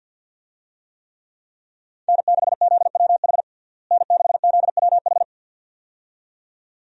Morse code, heard as a keyed tone, about 51 wpm, 700 Hz. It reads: N6ZWH N6ZWH
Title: USA Call Signs - 5 Characters - Repeated 2x - 50wpm